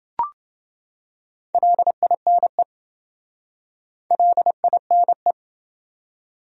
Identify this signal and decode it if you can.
Morse code, heard as a keyed tone, about 28 wpm, 700 Hz. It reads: LINE LINE